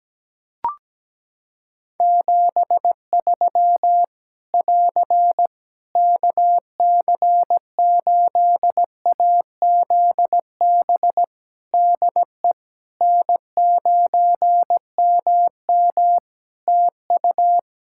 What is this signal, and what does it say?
Morse code, heard as a keyed tone, about 17 wpm, 700 Hz. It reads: 73 <AR> KC8AZB DE N9MM TU
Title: K1USN SST QSO Practice 17wpm